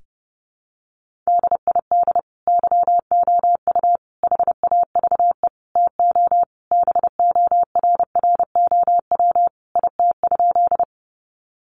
Morse code, heard as a keyed tone, about 30 wpm, 700 Hz. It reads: DID YOU HAVE TO BORROW IT?
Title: Sentences from Top 1500 Words 30wpm